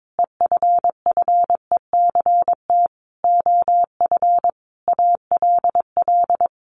Morse code, heard as a keyed tone, about 22 wpm, 700 Hz. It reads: EFFECT OF ALL
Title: Sentences from Top 1400 Words 22wpm